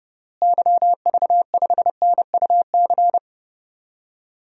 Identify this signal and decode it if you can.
Morse code, heard as a keyed tone, about 30 wpm, 700 Hz. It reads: YV5NUC
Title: DXCC-100 Call Signs - 6 Characters - Repeated 2x - 30wpm